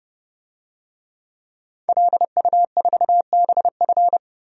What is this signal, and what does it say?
Morse code, heard as a keyed tone, about 32 wpm, 700 Hz. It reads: LU4BF
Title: DXCC-100 Call Signs - 5 Characters - 30wpm